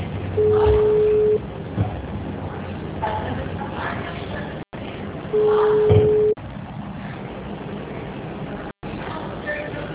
Ringback tone (Europe, 425 Hz), with noise.